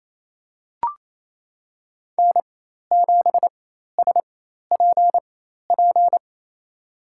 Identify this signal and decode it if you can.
Morse code, heard as a keyed tone, about 28 wpm, spaced as for 13 wpm, 690 Hz. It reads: N7SPP
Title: USA Call Signs - 5 Characters - 4x Character Spacing - 28wpm